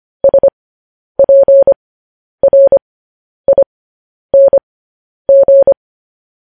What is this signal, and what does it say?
Morse code, heard as a keyed tone, about 25 wpm, spaced as for 10 wpm, 550 Hz. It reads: SPRING
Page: news-headlines-in-morse-code-at-10-wpm